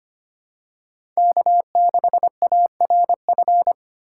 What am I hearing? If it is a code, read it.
Morse code, heard as a keyed tone, about 25 wpm, 700 Hz. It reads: K6ARF
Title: USA Call Signs - 6x Character Spacing - 25wpm